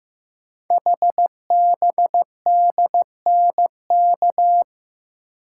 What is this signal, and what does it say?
Morse code, heard as a keyed tone, about 15 wpm, 700 Hz. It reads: HBDNK
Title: Code Groups - 5 Letters - 15wpm